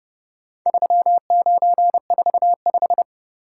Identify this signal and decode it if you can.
Morse code, heard as a keyed tone, about 31 wpm, 700 Hz. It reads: 3945